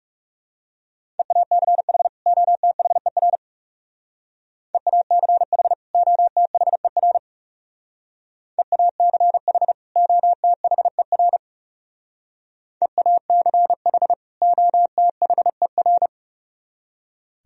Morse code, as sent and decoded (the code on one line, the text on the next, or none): . .- -.-. .... / --- - .... .-. / . .- -.-. .... / --- - .... .-. / . .- -.-. .... / --- - .... .-. / . .- -.-. .... / --- - .... .-.
EACH OTHER EACH OTHER EACH OTHER EACH OTHER